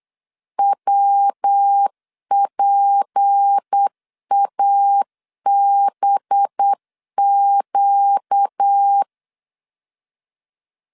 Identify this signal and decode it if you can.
Morse code, heard as a keyed tone, about 8 wpm, 800 Hz. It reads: WPABQ
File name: LEZIONI / 6 PKX0 / 6 P 8wpm 3025